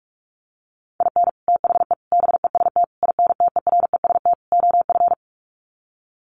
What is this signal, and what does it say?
Morse code, heard as a keyed tone, about 44 wpm, 700 Hz. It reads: IN THE BEST INTEREST OF
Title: Sets of 5 Words Encore 45wpm